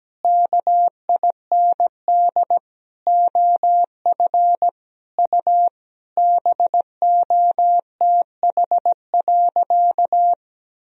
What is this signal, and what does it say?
Morse code, heard as a keyed tone, about 17 wpm, 700 Hz. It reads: KIND OF U BOTH.
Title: Sentences from Top 300 Words and Abbreviations 17wpm